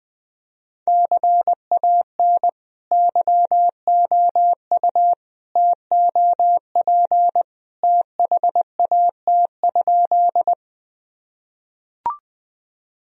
Morse code, heard as a keyed tone, about 20 wpm, 700 Hz. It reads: CAN YOU TOP THAT?
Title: Sentences from Top 1000 Words - Review Part 2 - 20wpm